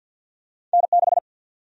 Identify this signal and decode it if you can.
Morse code, heard as a keyed tone, about 53 wpm, 700 Hz. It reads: N6
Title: USA Call Sign Prefixes - Repeated 2x - 50wpm